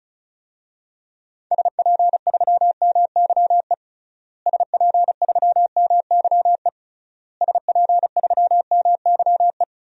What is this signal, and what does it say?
Morse code, heard as a keyed tone, about 36 wpm, 700 Hz. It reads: SP3MYE SP3MYE SP3MYE
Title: DXCC-100 Call Signs - 6 Characters - Repeated 3x - 35wpm